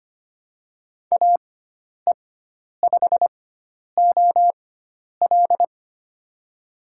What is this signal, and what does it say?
Morse code, heard as a keyed tone, about 25 wpm, spaced as for 10 wpm, 700 Hz. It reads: AE5OL
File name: Lesson 16g - Letter c - Call Signs 25wpm